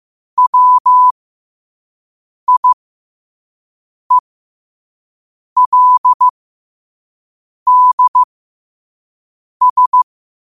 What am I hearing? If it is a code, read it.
Morse code, heard as a keyed tone, about 15 wpm, spaced as for 5 wpm, 1000 Hz. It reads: WIELDS
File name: qotdfiles / 5wpm